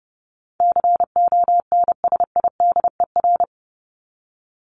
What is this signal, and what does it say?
Morse code, heard as a keyed tone, about 30 wpm, 700 Hz. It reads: CONSIDER
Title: ED Suffix - Top 100 Words with Base Words 30wpm